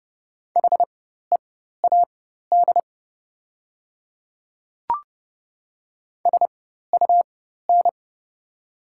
Morse code, heard as a keyed tone, about 30 wpm, spaced as for 14 wpm, 700 Hz. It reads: HEAD SUN
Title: 4x Character Spacing - Top 500 Words - 30wpm